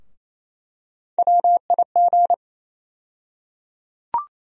Morse code, heard as a keyed tone, about 28 wpm, 700 Hz. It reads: WIG
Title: USA Call Sign Postfixes - 3 Letters 28wpm